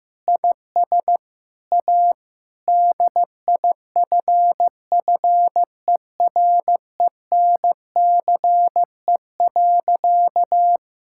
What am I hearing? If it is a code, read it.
Morse code, heard as a keyed tone, about 15 wpm, 700 Hz. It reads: IS A DIFFERENCE.
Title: Sentences from Top 1000 Words - Review Part 5 - 15wpm